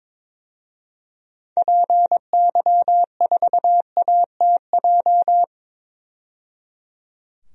Morse code, heard as a keyed tone, about 22 wpm, 700 Hz. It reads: PY4ATJ
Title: DXCC-100 Call Signs - 6 Characters - 22wpm